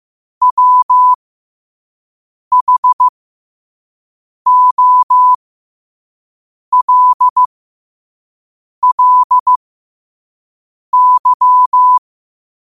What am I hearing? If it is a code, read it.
Morse code, heard as a keyed tone, about 15 wpm, spaced as for 5 wpm, 1000 Hz. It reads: WHOLLY
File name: qotdfiles / 5wpm